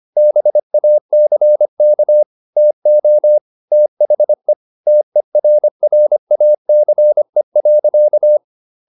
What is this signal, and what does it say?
Morse code, heard as a keyed tone, about 25 wpm, 600 Hz. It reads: BACK TO THE TERRACE.
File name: the old man and the sea part 1 25wpm